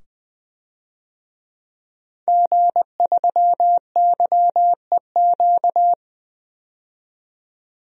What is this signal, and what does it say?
Morse code, heard as a keyed tone, about 20 wpm, 700 Hz. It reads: G3YEQ
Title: DXCC-100 Call Signs - 5 Characters - 20wpm